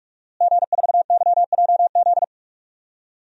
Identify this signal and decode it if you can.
Morse code, heard as a keyed tone, about 46 wpm, 700 Hz. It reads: G4YJZ
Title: DXCC-100 Call Signs - 5 Characters - Repeated 3x - 45wpm